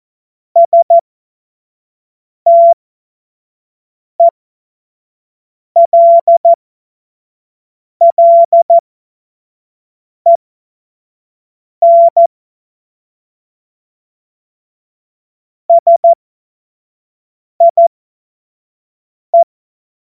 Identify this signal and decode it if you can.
Morse code, heard as a keyed tone, about 14 wpm, spaced as for 5 wpm, 680 Hz. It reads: STELLEN SIE
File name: Morsekurs / Witze / Witz 17. Januar 2023 25 F70